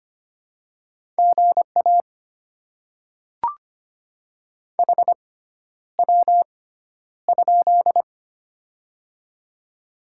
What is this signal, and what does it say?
Morse code, heard as a keyed tone, about 25 wpm, spaced as for 11 wpm, 700 Hz. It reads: GA HW?
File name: Top 20 QSO Elements - 6x Character Spacing - 25wpm